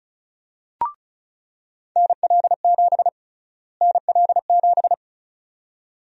Morse code, heard as a keyed tone, about 36 wpm, 700 Hz. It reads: NL7 NL7
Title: USA Call Sign Prefixes - Repeated 2x - 35wpm